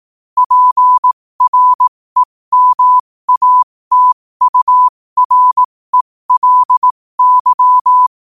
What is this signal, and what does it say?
Morse code, heard as a keyed tone, about 18 wpm, spaced as for 15 wpm, 1000 Hz. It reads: PREMATURELY